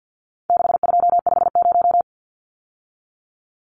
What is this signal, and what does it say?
Morse code, heard as a keyed tone, about 50 wpm, 700 Hz. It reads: 6250
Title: Numbers - 4-Digit 50wpm